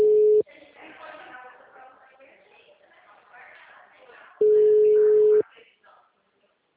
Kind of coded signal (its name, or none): telephone call progress tone